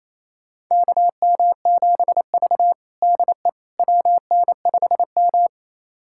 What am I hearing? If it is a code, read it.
Morse code, heard as a keyed tone, about 28 wpm, 700 Hz. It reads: KM7V DE WN5M
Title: USA Callsign DE Callsign - 4 Characters - 28wpm